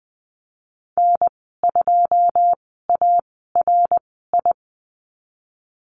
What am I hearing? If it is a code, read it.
Morse code, heard as a keyed tone, about 19 wpm, spaced as for 14 wpm, 700 Hz. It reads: N2ARI
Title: 088 Lesson 15g - Number 5 - Call Signs 20wpm